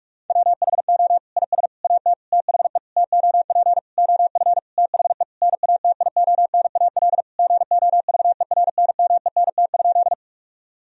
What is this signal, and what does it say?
Morse code, heard as a keyed tone, about 47 wpm, 700 Hz. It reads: WHO IS AT THE TOP OF THE NATIONAL GOVERNMENT?
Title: Sentences from Top 1000 Words - Review Part 4 - 45wpm